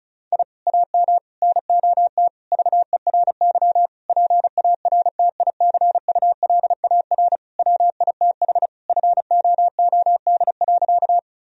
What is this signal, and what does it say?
Morse code, heard as a keyed tone, about 34 wpm, 700 Hz. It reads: I AM NOT VERY PARTICULAR WITH FOOD.
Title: Sentences from Top 1000 Words - Review Part 4 - 35wpm